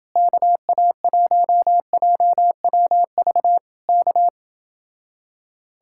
Morse code, heard as a keyed tone, about 28 wpm, 700 Hz. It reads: KA1JWV K